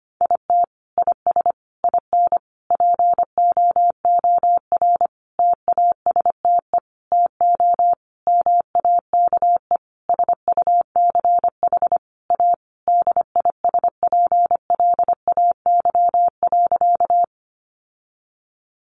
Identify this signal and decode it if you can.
Morse code, heard as a keyed tone, about 25 wpm, 700 Hz. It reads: IT IS IN POOR TASTE TO MAKE SUCH A DISPLAY.
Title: Sentences from Top 1200 Words 25wpm